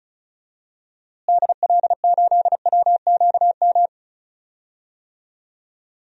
Morse code, heard as a keyed tone, about 36 wpm, 700 Hz. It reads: DL8WQM
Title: DXCC-100 Call Signs - 6 Characters - Repeated 2x - 35wpm